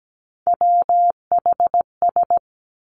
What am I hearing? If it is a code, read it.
Morse code, heard as a keyed tone, about 17 wpm, 700 Hz. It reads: WHS
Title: Three Challenging Letter-Number Combinations 17wpm